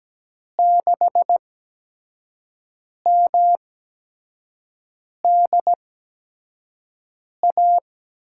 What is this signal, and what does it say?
Morse code, heard as a keyed tone, about 17 wpm, spaced as for 5 wpm, 700 Hz. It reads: 6MDA
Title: USA Call Signs - 6 Characters - 8x Character Spacing - 17wpm